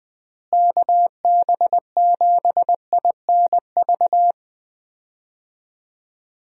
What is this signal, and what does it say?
Morse code, heard as a keyed tone, about 21 wpm, 700 Hz. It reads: KB7INV
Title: USA Call Signs - 6 Characters - Repeated 2x - 20wpm